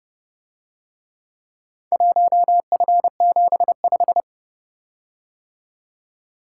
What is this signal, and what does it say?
Morse code, heard as a keyed tone, about 31 wpm, 700 Hz. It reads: 1F75